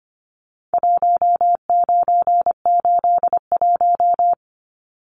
Morse code, heard as a keyed tone, about 25 wpm, 700 Hz. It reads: 1981
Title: Numbers - Contest Style - 4-Digit 25wpm